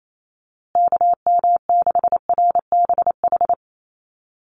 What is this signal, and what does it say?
Morse code, heard as a keyed tone, about 28 wpm, 690 Hz. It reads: KM6RBH
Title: USA Call Signs Repeated 3x 28wpm